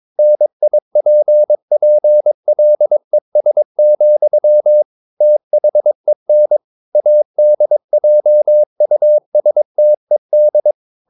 Morse code, heard as a keyed tone, about 22 wpm, 600 Hz. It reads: NIPPLES, THEN ADJUSTED
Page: Romanen Never af Ken Follet på engelsk.